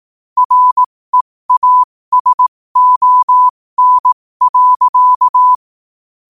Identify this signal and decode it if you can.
Morse code, heard as a keyed tone, about 18 wpm, spaced as for 15 wpm, 1000 Hz. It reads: REASON.
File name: qotdfiles / 15wpm